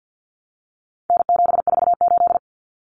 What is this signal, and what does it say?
Morse code, heard as a keyed tone, about 50 wpm, 700 Hz. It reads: N748
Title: Common Cut Numbers - 4-Digit 50wpm